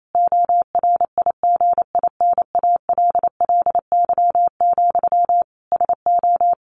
Morse code, heard as a keyed tone, about 28 wpm, 700 Hz. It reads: ORIGINALLY, SO
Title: A Sound Of Thunder 28wpm